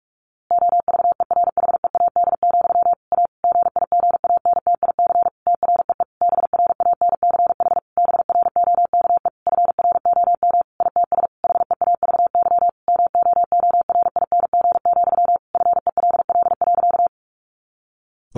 Morse code, heard as a keyed tone, about 44 wpm, 700 Hz. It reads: OVERHEAD, A GIGANTIC TREE BRANCH BROKE FROM ITS HEAVY MOORING, FELL.
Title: A Sound Of Thunder 45wpm